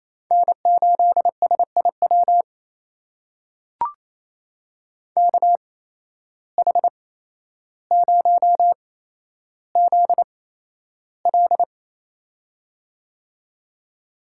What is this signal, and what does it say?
Morse code, heard as a keyed tone, about 28 wpm, spaced as for 12 wpm, 700 Hz. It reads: N8SIW KH0ZL
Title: USA Call Signs - 5 Characters - 8x Character Spacing - 28wpm